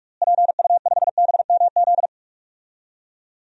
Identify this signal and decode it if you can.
Morse code, heard as a keyed tone, about 46 wpm, 700 Hz. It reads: PU5BMZ
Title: DXCC-100 Call Signs - 6 Characters - Repeated 3x - 45wpm